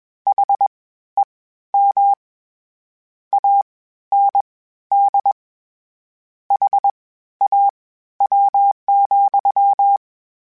Morse code, heard as a keyed tone, about 21 wpm, spaced as for 13 wpm, 800 Hz. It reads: HEM AND HAW,
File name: Misc / Misc-mp3s / WordPairs sp @20